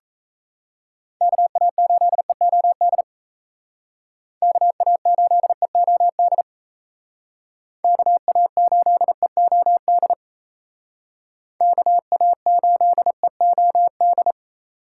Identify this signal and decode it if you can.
Morse code, heard as a keyed tone, about 35 wpm, 700 Hz. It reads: KA8EOD KA8EOD KA8EOD KA8EOD